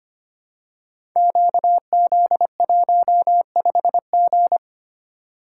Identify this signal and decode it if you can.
Morse code, heard as a keyed tone, about 25 wpm, 700 Hz. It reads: QZ15G
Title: Code Groups - 5 Letter:Numbers - 25wpm